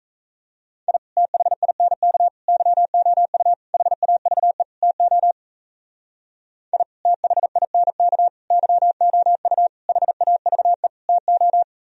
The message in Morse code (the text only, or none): I THINK YOU HAVE TO I THINK YOU HAVE TO